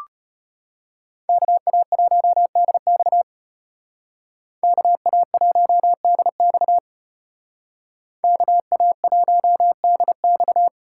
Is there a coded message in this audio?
Yes: Morse code from 1.3 to 10.7 s, heard as a keyed tone, about 32 wpm, 700 Hz. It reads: KA1DX KA1DX KA1DX